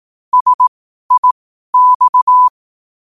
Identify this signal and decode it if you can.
Morse code, heard as a keyed tone, about 18 wpm, spaced as for 12 wpm, 1000 Hz. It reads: SIX
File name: qotdfiles / 12wpm